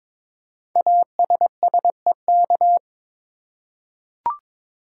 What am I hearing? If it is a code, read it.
Morse code, heard as a keyed tone, about 22 wpm, 700 Hz. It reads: ASSEK
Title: Code Groups - 5 Letters - 22wpm